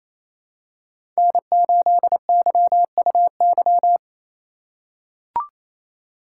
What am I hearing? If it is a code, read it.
Morse code, heard as a keyed tone, about 27 wpm, 700 Hz. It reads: N8YUY
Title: USA Call Signs - 5 Characters - 4x Character Spacing - 28wpm